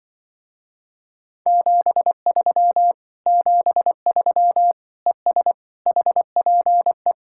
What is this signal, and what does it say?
Morse code, heard as a keyed tone, about 23 wpm, 700 Hz. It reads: 73 73 ES HPE
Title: Speed-Racing Common QSO Phrases - 20wpm